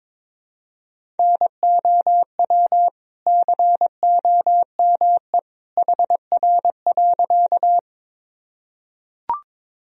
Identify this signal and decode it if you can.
Morse code, heard as a keyed tone, about 22 wpm, 700 Hz. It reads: NOW COME HR.
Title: Sentences from Top 200 Words and Abbreviations 22wpm